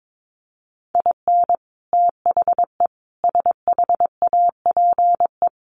Morse code, heard as a keyed tone, about 22 wpm, 700 Hz. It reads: IN THE SHAPE
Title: Sets of 5 Words Encore 22wpm